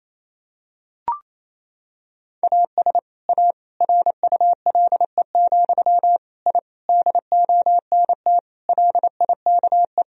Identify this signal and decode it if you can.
Morse code, heard as a keyed tone, about 28 wpm, 700 Hz. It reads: AS A RULE, I DONT LIKE
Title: Sentences from Top 1000 Words - Review Part 2 - 28wpm